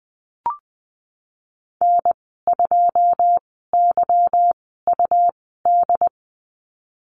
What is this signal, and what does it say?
Morse code, heard as a keyed tone, about 20 wpm, 700 Hz. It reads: N 2 Y U D
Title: 154 Lesson 25g - Letter p - Call Signs 20wpm